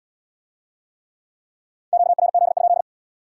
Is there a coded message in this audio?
Yes: Morse code from 1.9 to 2.8 s, heard as a keyed tone, about 75 wpm, 700 Hz. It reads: QRZ?